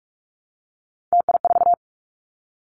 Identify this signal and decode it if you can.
Morse code, heard as a keyed tone, about 46 wpm, 700 Hz. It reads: TI4
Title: Callsign Prefixes for Top 100 DXCC - 45wpm